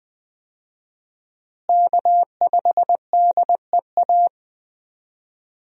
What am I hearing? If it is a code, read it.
Morse code, heard as a keyed tone, about 20 wpm, 700 Hz. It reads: K5DEA
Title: USA Call Signs - 5 Characters - Repeated 2x - 20wpm